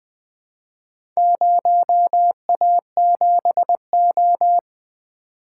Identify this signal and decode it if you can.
Morse code, heard as a keyed tone, about 20 wpm, 700 Hz. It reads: 0A7O
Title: Four Letter-Number Combinations - 20wpm